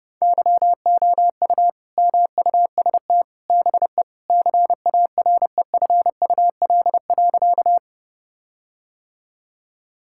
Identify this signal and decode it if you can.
Morse code, heard as a keyed tone, about 30 wpm, 700 Hz. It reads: YOU MUST BE CAREFUL.
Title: Sentences from Top 1600 Words 30wpm